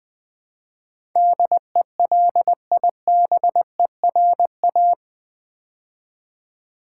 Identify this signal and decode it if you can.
Morse code, heard as a keyed tone, about 21 wpm, 700 Hz. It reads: DELIBERA
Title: Build A Word - 4 Letters At A Time - 12-Letter Words - 20wpm